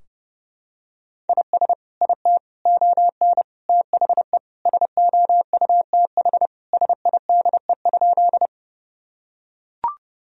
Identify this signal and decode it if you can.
Morse code, heard as a keyed tone, about 30 wpm, 700 Hz. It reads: IS IT ON THE SOUTH SIDE?
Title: Sentences from Top 1000 Words - Review Part 2 - 30wpm